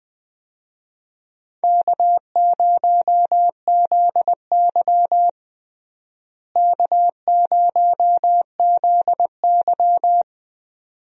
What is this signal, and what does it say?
Morse code, heard as a keyed tone, about 20 wpm, 700 Hz. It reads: K0ZY K0ZY
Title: USA Call Signs - 4 Characters - Repeated 2x - 20wpm